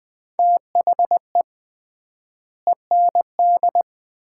Morse code, heard as a keyed tone, about 20 wpm, 700 Hz. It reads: THE END
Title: 3x Word-Spacing - Sets of 5 Words - 20wpm